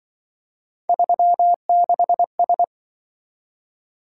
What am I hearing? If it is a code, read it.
Morse code, heard as a keyed tone, about 23 wpm, 700 Hz. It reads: 36S
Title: Speed-Racing - Three Letter-Number Combinations - 20wpm